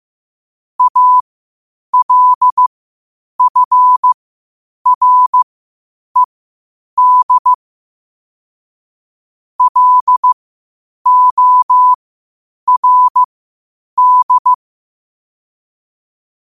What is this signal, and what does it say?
Morse code, heard as a keyed tone, about 15 wpm, spaced as for 8 wpm, 1000 Hz. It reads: ALFRED LORD
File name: qotdfiles / 8wpm